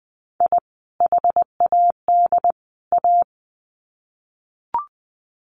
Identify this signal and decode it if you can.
Morse code, heard as a keyed tone, about 20 wpm, 700 Hz. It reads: I HAD A
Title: Sets of 3 Words 20wpm